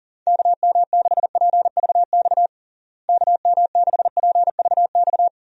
Morse code, heard as a keyed tone, about 37 wpm, 700 Hz. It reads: KM6PVX KM6PVX
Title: USA Call Signs - 6 Characters - Repeated 3x - 40wpm